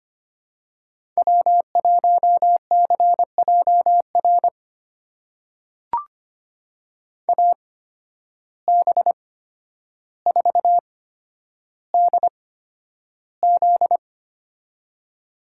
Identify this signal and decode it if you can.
Morse code, heard as a keyed tone, about 25 wpm, spaced as for 11 wpm, 700 Hz. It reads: W1CJR AB4DZ
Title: USA Call Signs - 5 Characters - 8x Character Spacing - 25wpm